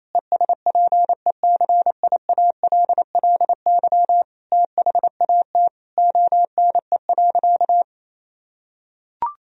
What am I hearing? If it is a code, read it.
Morse code, heard as a keyed tone, about 28 wpm, 700 Hz. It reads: ESPECIALLY THAT ONE.
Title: Sentences from Top 1000 Words - Review Part 4 - 28wpm